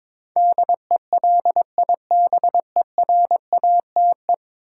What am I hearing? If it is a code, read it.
Morse code, heard as a keyed tone, about 22 wpm, 700 Hz. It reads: DELIBERATE
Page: morse-code-ninja-practice-22wpm